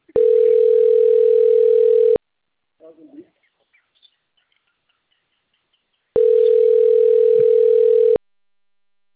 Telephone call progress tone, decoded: ringback tone

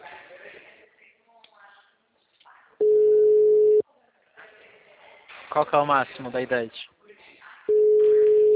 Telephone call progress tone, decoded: ringback tone

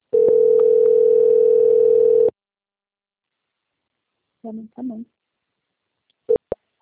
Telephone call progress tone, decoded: ringback tone

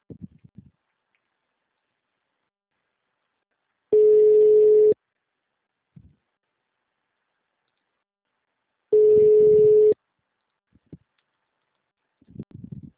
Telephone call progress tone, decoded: ringback tone

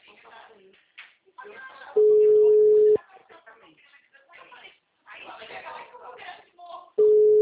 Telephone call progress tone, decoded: ringback tone

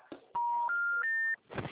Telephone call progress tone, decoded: special information tone

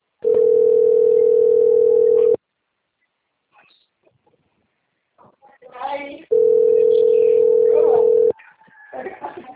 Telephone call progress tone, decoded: ringback tone

